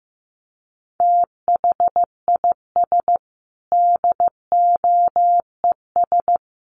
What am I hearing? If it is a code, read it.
Morse code, heard as a keyed tone, about 15 wpm, 700 Hz. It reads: THIS DOES
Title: Sets of 5 Words Encore2 15wpm